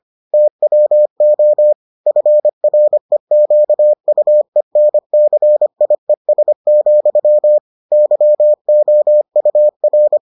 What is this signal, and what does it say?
Morse code, heard as a keyed tone, about 25 wpm, 600 Hz. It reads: TWO FREQUENCIES, YOUR